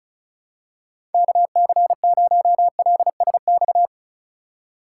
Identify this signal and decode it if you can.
Morse code, heard as a keyed tone, about 35 wpm, 700 Hz. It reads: KC0LSX
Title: USA Call Signs - 4x Character Spacing - 35wpm